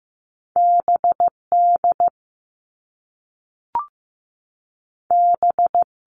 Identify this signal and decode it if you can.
Morse code, heard as a keyed tone, about 15 wpm, 700 Hz. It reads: B D B
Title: All Cut Numbers - 2-Digit 15wpm